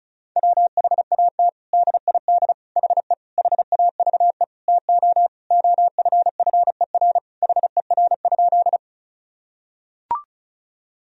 Morse code, heard as a keyed tone, about 36 wpm, 700 Hz. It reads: WHAT DID HE HAVE TO OFFER HER?
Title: Sentences from Top 1000 Words - Review Part 4 - 35wpm